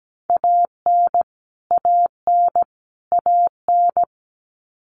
Morse code, heard as a keyed tone, about 17 wpm, 700 Hz. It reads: AN AN AN
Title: Top 500 Words Repeated 17wpm